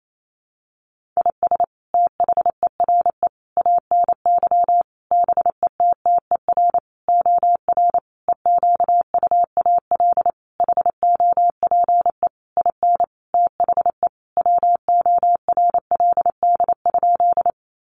Morse code, heard as a keyed tone, about 28 wpm, 700 Hz. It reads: IS THERE ANY BETTER OR EQUAL HOPE IN THE WORLD?
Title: Sentences from Top 1400 Words 28wpm